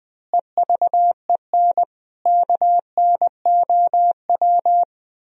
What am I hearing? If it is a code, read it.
Morse code, heard as a keyed tone, about 21 wpm, 700 Hz. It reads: EVEN KNOW